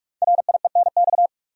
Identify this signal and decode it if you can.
Morse code, heard as a keyed tone, about 44 wpm, 700 Hz. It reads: RIENX
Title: Code Groups - 5 Letters - 45wpm